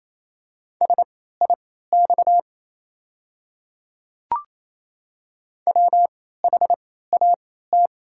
Morse code, heard as a keyed tone, about 28 wpm, spaced as for 16 wpm, 700 Hz. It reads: SIX WHAT
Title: 3x Character Spacing - Top 500 Words - 28wpm